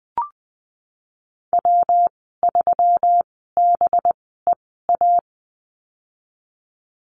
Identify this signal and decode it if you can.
Morse code, heard as a keyed tone, about 20 wpm, spaced as for 15 wpm, 700 Hz. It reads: W3BEA
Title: 213 Lesson 34g - Number 8 - Call Signs 20wpm